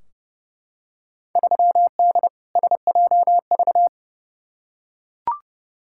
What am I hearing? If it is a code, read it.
Morse code, heard as a keyed tone, about 30 wpm, 700 Hz. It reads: 3D SJV
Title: Field Day Exchanges - 30wpm